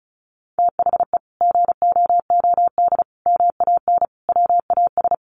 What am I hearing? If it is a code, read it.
Morse code, heard as a keyed tone, about 34 wpm, 700 Hz. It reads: THE GOOD MAN WAS
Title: Sentences from Top 1500 Words 35wpm